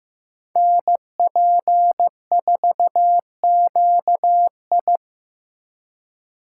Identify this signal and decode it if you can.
Morse code, heard as a keyed tone, about 16 wpm, 700 Hz. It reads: NP4QI
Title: USA Call Signs - 5 Characters - 2x Character Spacing - 15wpm